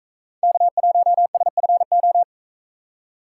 Morse code, heard as a keyed tone, about 40 wpm, 700 Hz. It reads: K1SFO